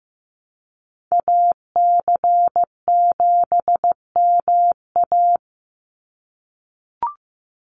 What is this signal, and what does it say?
Morse code, heard as a keyed tone, about 15 wpm, 700 Hz. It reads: AC7MA
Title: NA SOTA Activator Callsigns 15wpm